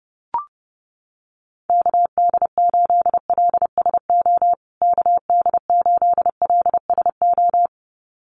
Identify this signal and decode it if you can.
Morse code, heard as a keyed tone, about 30 wpm, 700 Hz. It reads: KD8LSO KD8LSO
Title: USA Call Signs Repeated 2x 30wpm